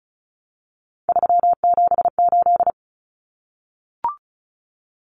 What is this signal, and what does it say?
Morse code, heard as a keyed tone, about 34 wpm, 700 Hz. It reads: 378
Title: Common Cut Numbers - 3-Digit 35wpm